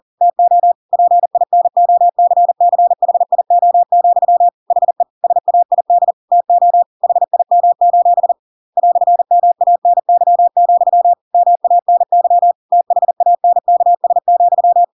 Morse code, heard as a keyed tone, about 40 wpm, 700 Hz. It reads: TO PINOCCHIO, HE SAID TO HIM: "MANY, MANY THANKS,
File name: Pinocchio / Pinocchio ch5 40 40 ik1hge